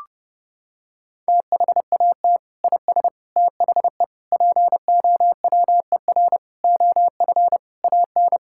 Morse code, heard as a keyed tone, about 32 wpm, 700 Hz. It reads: THAT IS THE POWER OF AN
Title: Sentences from Top 1000 Words - Review Part 5 - 30wpm